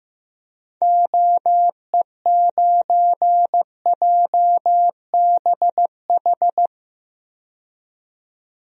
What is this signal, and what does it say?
Morse code, heard as a keyed tone, about 16 wpm, 700 Hz. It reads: OE9JBH
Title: DXCC-100 Call Signs - 6 Characters - 15wpm